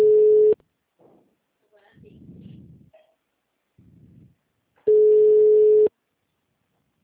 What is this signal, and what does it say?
Ringback tone (Europe, 425 Hz), with speech.